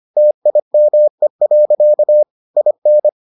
Morse code, heard as a keyed tone, about 25 wpm, 600 Hz. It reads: TIME. IN